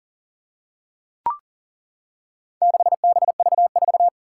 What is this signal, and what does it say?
Morse code, heard as a keyed tone, about 40 wpm, 700 Hz. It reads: 6BV4
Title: Four Challenging Letter-Number Combinations 40wpm